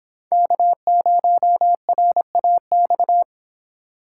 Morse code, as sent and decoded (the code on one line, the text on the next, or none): -.- ----- .-. .- -..-
K0RAX